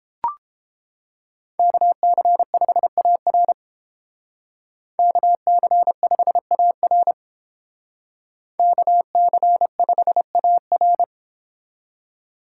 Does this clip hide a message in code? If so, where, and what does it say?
Yes: Morse code from 1.6 to 11.0 s, heard as a keyed tone, about 31 wpm, 700 Hz. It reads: KC5AR KC5AR KC5AR